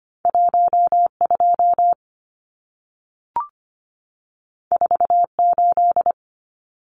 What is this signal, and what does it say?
Morse code, heard as a keyed tone, about 25 wpm, 700 Hz. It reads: 12 48
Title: Numbers - 2-Digit 25wpm